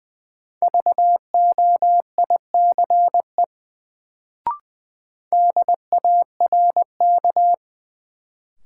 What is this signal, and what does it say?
Morse code, heard as a keyed tone, about 20 wpm, 700 Hz. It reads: VOICE DARK
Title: Sentences from Top 500 Words - New Word Drill - 20wpm